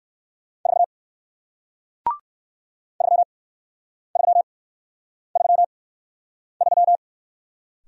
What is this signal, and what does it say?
Morse code, heard as a keyed tone, about 57 wpm, spaced as for 9 wpm, 700 Hz. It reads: 4 3333